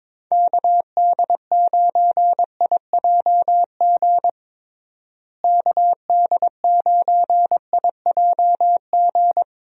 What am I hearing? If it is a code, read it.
Morse code, heard as a keyed tone, about 22 wpm, 700 Hz. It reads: KD9IJG KD9IJG
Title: USA Call Signs - 6 Characters - Repeated 3x - 22wpm